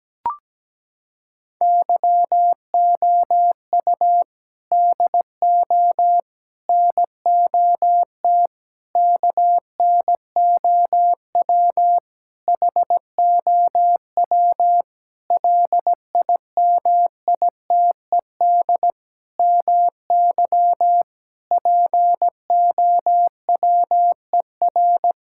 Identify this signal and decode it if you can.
Morse code, heard as a keyed tone, about 17 wpm, 700 Hz. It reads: YOU DO NOT KNOW HOW LIMITED MY POWER
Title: Sentences from Top 1000 Words - Review Part 6 - 17wpm